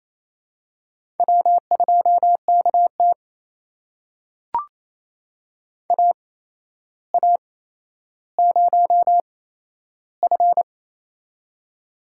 Morse code, heard as a keyed tone, about 28 wpm, spaced as for 12 wpm, 700 Hz. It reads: W2KT AA0F
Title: USA Call Signs - 4 Characters - 8x Character Spacing - 28wpm